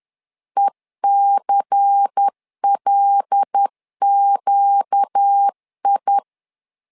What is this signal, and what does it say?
Morse code, heard as a keyed tone, about 10 wpm, 800 Hz. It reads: ECLQI
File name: LEZIONI / 5 LBVC / 5 LBVC 10wpm 2349